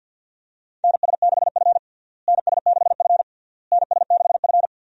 Morse code, heard as a keyed tone, about 52 wpm, 700 Hz. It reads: NS6F NS6F NS6F
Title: USA Call Signs - 4 Characters - Repeated 3x - 50wpm